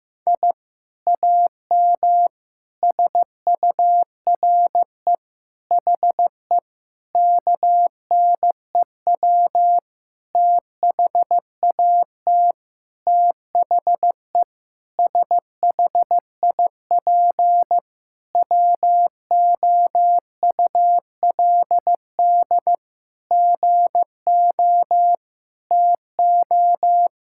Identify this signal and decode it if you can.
Morse code, heard as a keyed tone, about 16 wpm, 700 Hz. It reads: I AM SURE HE KNEW THAT THE SHIP WOULD GO TO
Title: Sentences from Top 1000 Words - Review Part 6 - 15wpm